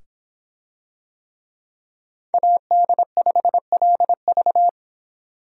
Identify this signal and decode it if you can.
Morse code, heard as a keyed tone, about 25 wpm, 700 Hz. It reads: AD5LV